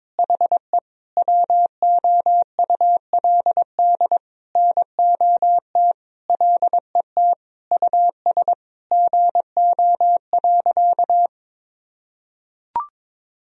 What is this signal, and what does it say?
Morse code, heard as a keyed tone, about 22 wpm, 700 Hz. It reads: HE WOULD NOT LET US GO.
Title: Sentences from Top 1000 Words - Review Part 2 - 22wpm